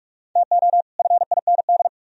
Morse code, heard as a keyed tone, about 38 wpm, 700 Hz. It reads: TO FIND